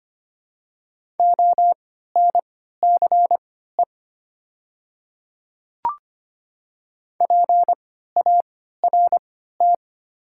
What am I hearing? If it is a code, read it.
Morse code, heard as a keyed tone, about 25 wpm, spaced as for 14 wpm, 690 Hz. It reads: ONCE PART
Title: 3x Character Spacing - Top 300 Words - 25wpm